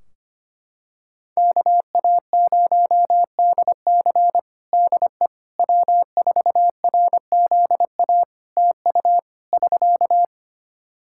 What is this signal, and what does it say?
Morse code, heard as a keyed tone, about 25 wpm, 700 Hz. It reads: KA0DC DE W4RZA TU <SK>
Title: Practice QSO - Exchange 4 - Ending - 25wpm